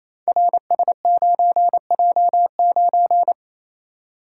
Morse code, heard as a keyed tone, about 28 wpm, 700 Hz. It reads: RS9J9
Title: Code Groups - 5 Letter:Numbers - 28wpm